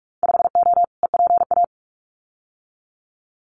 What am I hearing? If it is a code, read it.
Morse code, heard as a keyed tone, about 46 wpm, 700 Hz. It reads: 5O EPA
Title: Winter Field Day Exchanges Encore 45wpm